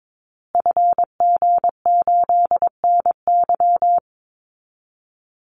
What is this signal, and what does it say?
Morse code, heard as a keyed tone, about 22 wpm, 700 Hz. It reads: FG8NY
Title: Call Signs Repeated 2x - 22wpm